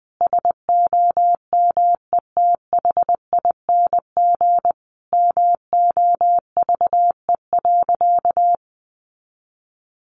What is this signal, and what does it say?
Morse code, heard as a keyed tone, about 20 wpm, 700 Hz. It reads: SOMETHING MOVE.
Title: Sentences from Top 1200 Words 20wpm